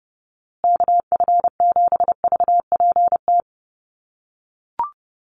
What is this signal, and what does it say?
Morse code, heard as a keyed tone, about 28 wpm, 700 Hz. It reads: KF7VPT
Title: USA Call Signs Repeated 3x 30wpm